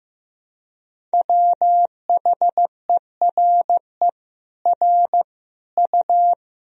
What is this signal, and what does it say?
Morse code, heard as a keyed tone, about 15 wpm, 700 Hz. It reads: WHERE R U
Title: Sentences from Top 400 Words and Abbreviations 15wpm